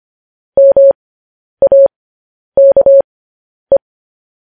Morse code, heard as a keyed tone, about 25 wpm, spaced as for 10 wpm, 550 Hz. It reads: MAKE
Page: news headlines in morse code at 10 wpm